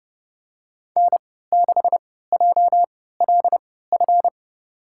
Morse code, heard as a keyed tone, about 30 wpm, spaced as for 17 wpm, 700 Hz. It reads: N6JLF
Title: USA Call Signs - 5 Characters - 3x Character Spacing - 30wpm